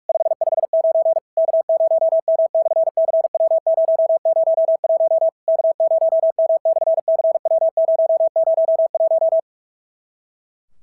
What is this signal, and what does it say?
Morse code, heard as a keyed tone, about 44 wpm, 650 Hz. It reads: 559 K0M/CW001 K0M/CW001
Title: SOTA QSO Practice 45wpm